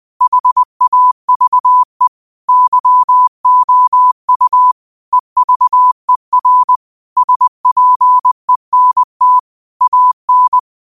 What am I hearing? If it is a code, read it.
Morse code, heard as a keyed tone, about 20 wpm, 1000 Hz. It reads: HAVE YOU EVER SPENT AN